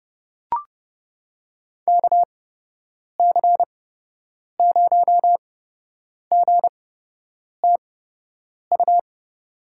Morse code, heard as a keyed tone, about 31 wpm, spaced as for 8 wpm, 700 Hz. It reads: KC0GTU
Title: USA Call Signs - 6 Characters - 8x Character Spacing - 30wpm